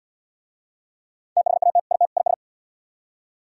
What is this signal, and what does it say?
Morse code, heard as a keyed tone, about 76 wpm, 700 Hz. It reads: THAT IT IS